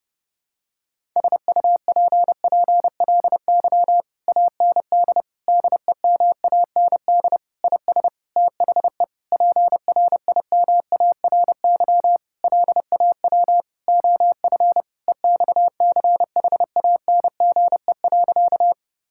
Morse code, heard as a keyed tone, about 30 wpm, 700 Hz. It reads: SUPPLY AND DEMAND IS THE PRIMARY LAW OF EXCHANGE.
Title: Sentences from Top 1000 Words - Review Part 6 - 30wpm